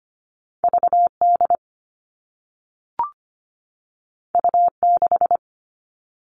Morse code, heard as a keyed tone, about 25 wpm, 700 Hz. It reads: V D U 6
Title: All Cut Numbers - 2-Digit 25wpm